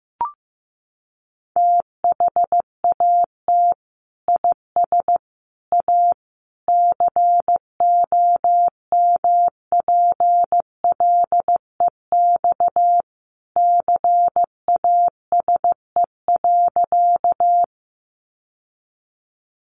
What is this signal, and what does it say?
Morse code, heard as a keyed tone, about 15 wpm, 700 Hz. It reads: THAT IS A COMPLEX CASE.
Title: Sentences from Top 900 Words 15wpm